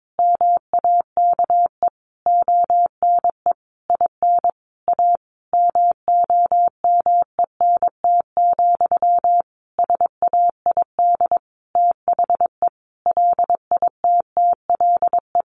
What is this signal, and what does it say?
Morse code, heard as a keyed tone, about 22 wpm, 700 Hz. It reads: MAKE ONE IN A MOMENT, SAID THE LITTLE
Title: Sentences from Top 1000 Words 22wpm